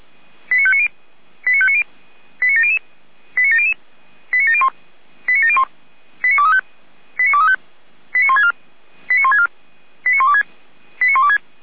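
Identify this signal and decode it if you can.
Five-tone selective call (ZVEI), heard as quick runs of five tones, tones of about 70 ms. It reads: 89590 89590 89700 89700 89701 89701 89235 89235 89164 89164 89126 89126
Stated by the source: ZVEI five-tone call received by SDR.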